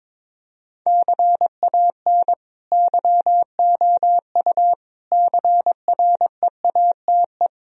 Morse code, heard as a keyed tone, about 22 wpm, 700 Hz. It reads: CAN YOU CREATE
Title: Sentences from Top 1000 Words - Review Part 3 - 22wpm